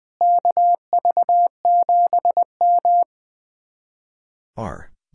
Morse code, heard as a keyed tone, about 20 wpm, 700 Hz. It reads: KV7M
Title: Speed-Racing Common QSO Phrases - 20wpm